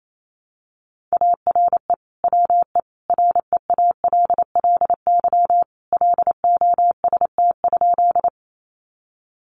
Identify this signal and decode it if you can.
Morse code, heard as a keyed tone, about 28 wpm, 700 Hz. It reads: ARE WE REALLY LOST?
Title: Sentences from Top 1400 Words 28wpm